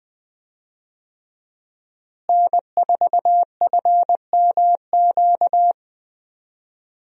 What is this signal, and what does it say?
Morse code, heard as a keyed tone, about 20 wpm, 700 Hz. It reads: N4FMQ
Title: USA Call Signs - 5 Characters - Repeated 3x - 20wpm